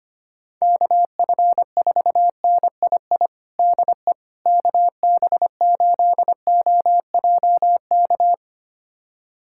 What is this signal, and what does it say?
Morse code, heard as a keyed tone, about 25 wpm, 700 Hz. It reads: KF4NII DE KB8OJK
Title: USA Callsign DE Callsign - 6 Characters - 25wpm